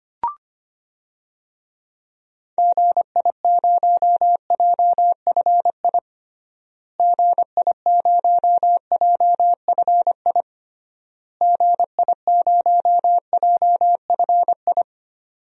Morse code, heard as a keyed tone, about 25 wpm, 700 Hz. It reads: GI0JFI GI0JFI GI0JFI
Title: DXCC-100 Call Signs - 6 Characters - Repeated 3x - 25wpm